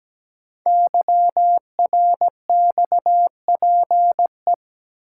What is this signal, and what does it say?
Morse code, heard as a keyed tone, about 17 wpm, 700 Hz. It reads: YRXPE
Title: Code Groups - 5 Letters - 17wpm